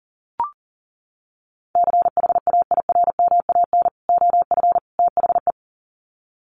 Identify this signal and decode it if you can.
Morse code, heard as a keyed tone, about 40 wpm, 700 Hz. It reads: CHAIRMAN OF THE
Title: Sets of 3 Words 40wpm